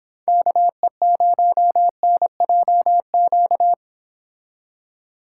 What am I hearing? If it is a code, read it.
Morse code, heard as a keyed tone, about 29 wpm, 700 Hz. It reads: KE0NJQ